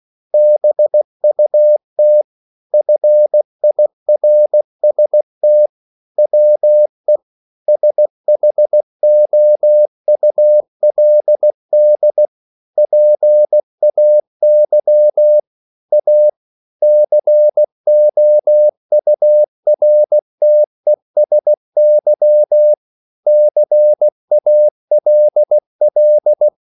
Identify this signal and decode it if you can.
Morse code, heard as a keyed tone, about 16 wpm, 600 Hz. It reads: BUT FIRST WE SHOULD PAY A COURTESY CALL